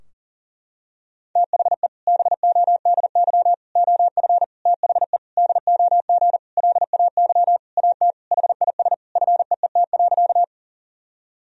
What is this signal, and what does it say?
Morse code, heard as a keyed tone, about 40 wpm, 700 Hz. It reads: THE BODY OF THE DOG LAY AT HIS FEET.
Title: Sentences from Top 1000 Words - Review Part 2 - 40wpm